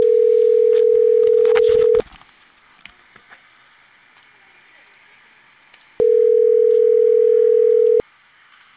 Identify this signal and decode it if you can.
Ringback tone (North America), 440 + 480 Hz.